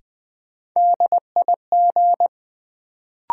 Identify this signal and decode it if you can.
Morse code, heard as a keyed tone, about 20 wpm, 700 Hz. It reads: DIG
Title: Sentences from Top 1600 Words - New Word Drill - 20wpm